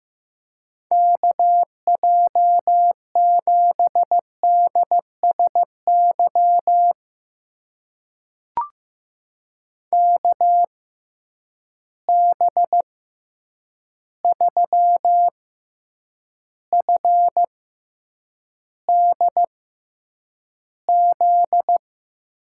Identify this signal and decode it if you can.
Morse code, heard as a keyed tone, about 15 wpm, spaced as for 8 wpm, 700 Hz. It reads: KJ7DSY KB3FDZ
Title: USA Call Signs - 6 Characters - 6x Character Spacing - 15wpm